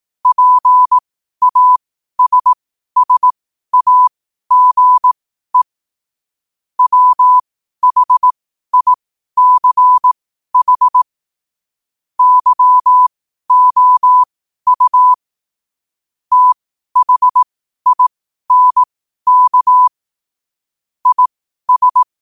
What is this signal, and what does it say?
Morse code, heard as a keyed tone, about 18 wpm, spaced as for 12 wpm, 1000 Hz. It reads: PASSAGE WHICH YOU THINK IS
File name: qotdfiles / 12wpm